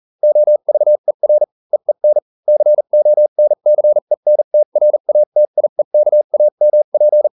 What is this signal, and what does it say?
Morse code, heard as a keyed tone, about 40 wpm, spaced as for 35 wpm, 600 Hz. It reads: OVER EEN CONCENTRATIEKAMP